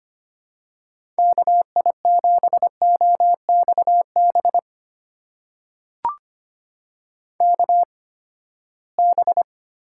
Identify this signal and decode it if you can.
Morse code, heard as a keyed tone, about 25 wpm, spaced as for 17 wpm, 700 Hz. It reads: KI7OXB KB
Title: USA Call Signs - 6 Characters - 8x Character Spacing - 25wpm